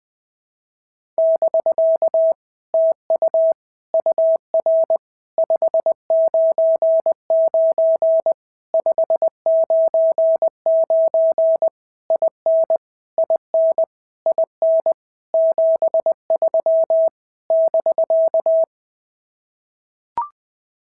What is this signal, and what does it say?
Morse code, heard as a keyed tone, about 20 wpm, 650 Hz. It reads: <BK> TU UR 599 599 IN IN IN 73 <BK>
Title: POTA QSO Practice - 20wpm